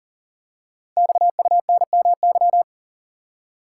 Morse code, heard as a keyed tone, about 43 wpm, 700 Hz. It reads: XUNMY
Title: Code Groups - 5 Letters - 40wpm